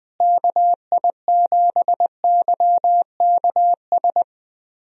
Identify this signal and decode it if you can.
Morse code, heard as a keyed tone, about 21 wpm, 700 Hz. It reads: KI7YKS
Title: USA Call Signs - 6 Characters - Repeated 3x - 20wpm